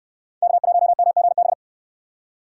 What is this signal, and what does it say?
Morse code, heard as a keyed tone, about 69 wpm, 700 Hz. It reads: K1MGB